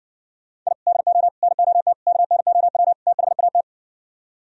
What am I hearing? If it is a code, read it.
Morse code, heard as a keyed tone, about 70 wpm, 700 Hz. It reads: I DO NOT KNOW THAT